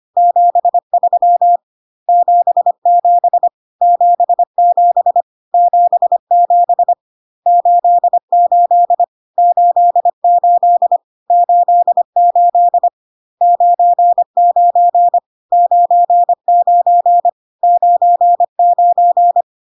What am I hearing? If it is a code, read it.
Morse code, heard as a keyed tone, about 25 wpm, 700 Hz. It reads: 73 77 77 77 88 88 88 99 99 99
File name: CW abbreviations alphabetical repeated 3 times 25 25wpm